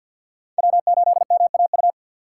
Morse code, heard as a keyed tone, about 51 wpm, 700 Hz. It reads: W8MAU
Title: USA Call Signs - 5 Characters - 2x Character Spacing - 50wpm